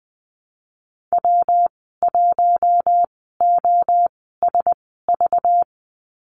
Morse code, heard as a keyed tone, about 20 wpm, spaced as for 14 wpm, 700 Hz. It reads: W1OSV
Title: 193 Lesson 31g - Letter v - Call Signs 20wpm